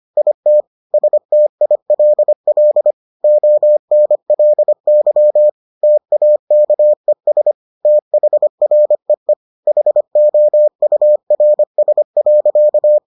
Morse code, heard as a keyed tone, about 25 wpm, 600 Hz. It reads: IT STILL ONLY TAKES THREE HOURS.